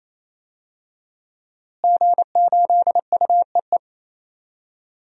Morse code, heard as a keyed tone, about 28 wpm, 700 Hz. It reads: G8UEE
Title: DXCC-100 Call Signs - 5 Characters - Repeated 3x - 28wpm